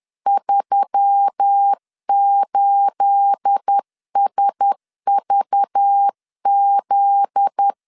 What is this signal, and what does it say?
Morse code, heard as a keyed tone, about 10 wpm, 800 Hz. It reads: 38SVZ
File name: CW / DET / 1 9 1500C S B 10wpm 4439